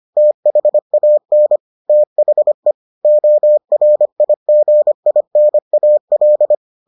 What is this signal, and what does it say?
Morse code, heard as a keyed tone, about 25 wpm, 600 Hz. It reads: THAN THE ORIGINAL